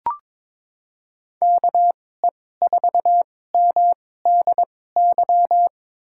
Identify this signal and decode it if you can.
Morse code, heard as a keyed tone, about 22 wpm, spaced as for 16 wpm, 700 Hz. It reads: KE4MDY
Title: USA Call Signs - 6 Characters - 2x Character Spacing - 22wpm